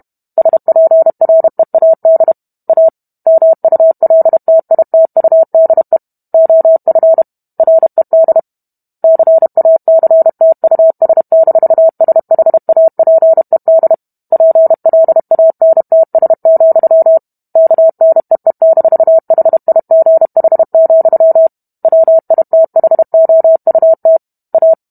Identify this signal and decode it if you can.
Morse code, heard as a keyed tone, about 32 wpm, 650 Hz. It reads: SPREAD A MULTITUDE OF RED CACTUS-SHAPED PLANTS, KNEE-HIGH, WITHOUT A